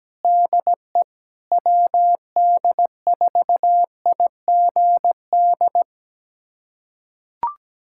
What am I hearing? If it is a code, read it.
Morse code, heard as a keyed tone, about 17 wpm, 700 Hz. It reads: DE WD4IGD
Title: USA Callsign DE Callsign 17wpm